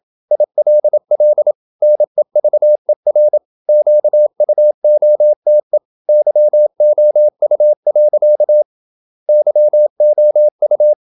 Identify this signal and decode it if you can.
Morse code, heard as a keyed tone, about 27 wpm, 600 Hz. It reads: ILL NEVER QUOTE YOU. YOU
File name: Never 0027wpm